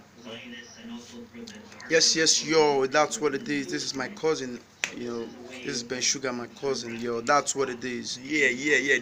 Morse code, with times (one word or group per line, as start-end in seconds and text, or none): none